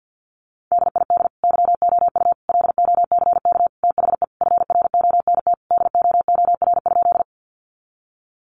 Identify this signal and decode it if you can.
Morse code, heard as a keyed tone, about 50 wpm, 700 Hz. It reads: DID YOU LOCK THE FRONT DOOR?
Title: Sentences from Top 1300 Words 50wpm